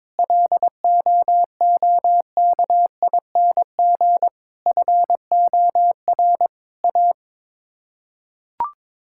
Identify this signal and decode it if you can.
Morse code, heard as a keyed tone, about 22 wpm, 700 Hz. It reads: LOOKING FOR A
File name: practice / 22wpm / Sets of 6 Words - 22wpm